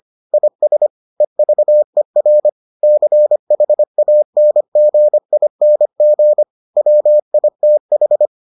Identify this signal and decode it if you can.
Morse code, heard as a keyed tone, about 25 wpm, 600 Hz. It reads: IS EVER CHANGING WITH